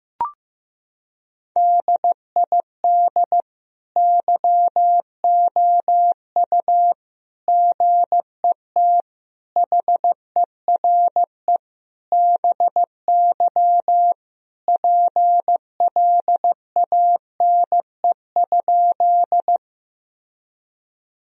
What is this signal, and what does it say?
Morse code, heard as a keyed tone, about 15 wpm, 700 Hz. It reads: DID YOU GET HERE BY PLANE?
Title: Sentences from Top 1000 Words - Review Part 3 - 15wpm